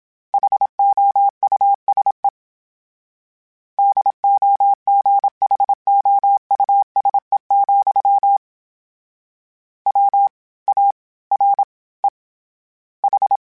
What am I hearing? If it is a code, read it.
Morse code, heard as a keyed tone, about 27 wpm, spaced as for 21 wpm, 800 Hz. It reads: HOUSE DOGHOUSE, WAREH
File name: Houses @25